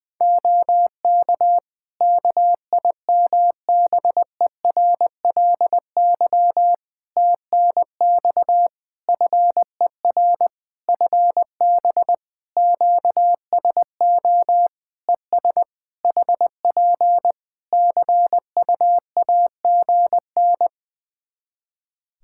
Morse code, heard as a keyed tone, about 20 wpm, 700 Hz. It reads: OK KIMBERLY TNX FER FB QSO ES HP CUAGN
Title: Practice QSO - Exchange 4 - Ending - 20wpm